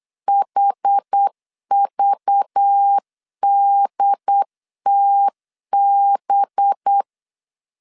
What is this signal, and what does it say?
Morse code, heard as a keyed tone, about 8 wpm, 800 Hz. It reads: HVDTB
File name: CW / DET / 1 6 1500C S B 8wpm 5142